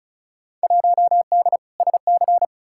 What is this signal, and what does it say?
Morse code, heard as a keyed tone, about 36 wpm, 700 Hz. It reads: 1D SC
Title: Field Day Exchanges - 35wpm